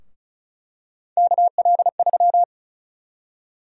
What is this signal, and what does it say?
Morse code, heard as a keyed tone, about 36 wpm, 700 Hz. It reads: KL3
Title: USA Call Sign Prefixes - Repeated 2x - 35wpm